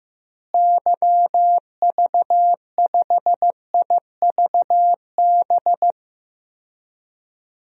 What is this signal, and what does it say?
Morse code, heard as a keyed tone, about 15 wpm, 700 Hz. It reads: YV5IVB
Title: DXCC-100 Call Signs - 6 Characters - Repeated 3x - 15wpm